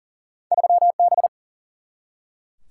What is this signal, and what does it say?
Morse code, heard as a keyed tone, about 40 wpm, 700 Hz. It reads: 3B
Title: Single-Letter Single-Number 40wpm